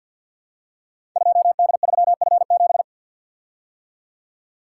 Morse code, heard as a keyed tone, about 47 wpm, 700 Hz. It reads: 2D3F7
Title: Code Groups - 5 Letter:Numbers - 50wpm